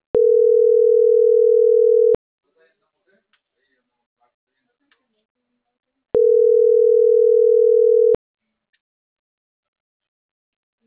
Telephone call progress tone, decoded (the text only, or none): ringback tone